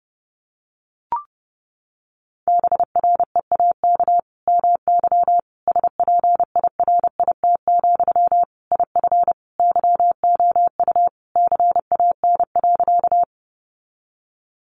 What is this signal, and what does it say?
Morse code, heard as a keyed tone, about 30 wpm, 700 Hz. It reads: BREAK MY SPIRIT, IF YOU CAN.
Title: Sentences from Top 1400 Words 30wpm